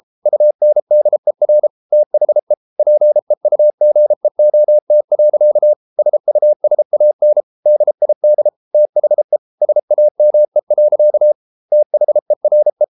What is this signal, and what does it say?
Morse code, heard as a keyed tone, about 33 wpm, 600 Hz. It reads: UNDER THE PEUGEOT. SUSAN DID THE SAME. THERE